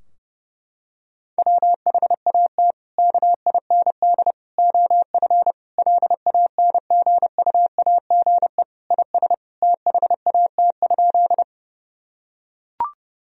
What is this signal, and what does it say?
Morse code, heard as a keyed tone, about 30 wpm, 700 Hz. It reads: WHAT KIND OF LANGUAGE IS THAT?
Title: Sentences from Top 1000 Words - Review Part 3 - 30wpm